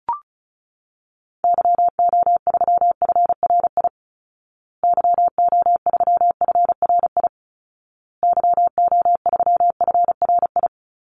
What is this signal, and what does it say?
Morse code, heard as a keyed tone, about 34 wpm, 700 Hz. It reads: YO3FRI YO3FRI YO3FRI
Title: Call Signs Repeated 2x - 35wpm